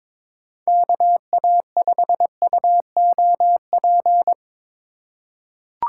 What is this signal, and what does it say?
Morse code, heard as a keyed tone, about 22 wpm, 700 Hz. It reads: KA5UOP
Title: USA Call Signs - 6 Characters - 4x Character Spacing - 22wpm